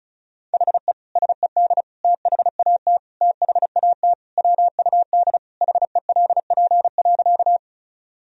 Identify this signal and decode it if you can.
Morse code, heard as a keyed tone, about 34 wpm, 700 Hz. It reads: HE SED THAT THAT WUD HELP.
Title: Sentences from Top 200 Words and Abbreviations 35wpm